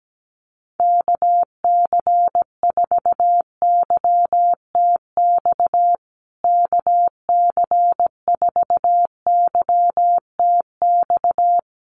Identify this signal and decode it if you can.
Morse code, heard as a keyed tone, about 17 wpm, 700 Hz. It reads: KC4YTX KC4YTX
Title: USA Call Signs Repeated 3x 17wpm